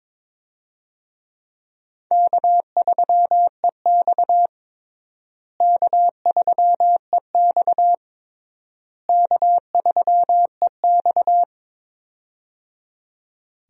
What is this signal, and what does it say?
Morse code, heard as a keyed tone, about 22 wpm, 700 Hz. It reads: K3EX K3EX K3EX
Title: USA Call Signs - 4 Characters - Repeated 3x - 22wpm